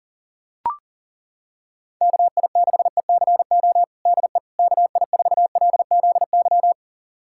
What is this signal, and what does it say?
Morse code, heard as a keyed tone, about 40 wpm, 700 Hz. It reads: KI6ECO DE KI4LZY
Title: USA Callsign DE Callsign - 6 Characters - 40wpm